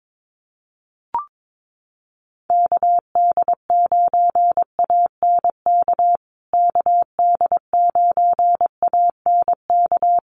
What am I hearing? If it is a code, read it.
Morse code, heard as a keyed tone, about 22 wpm, 700 Hz. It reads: KD9ANK KD9ANK
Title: USA Call Signs Repeated 3x 22wpm